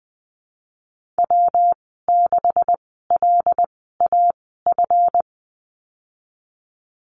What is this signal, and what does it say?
Morse code, heard as a keyed tone, about 20 wpm, spaced as for 15 wpm, 700 Hz. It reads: W6LAF
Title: 147 Lesson 24g - Letter y - Call Signs 20wpm